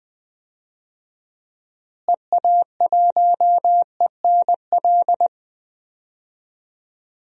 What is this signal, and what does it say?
Morse code, heard as a keyed tone, about 19 wpm, 700 Hz. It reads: EA1ENL
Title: DXCC-100 Call Signs - 6 Characters - Repeated 3x - 20wpm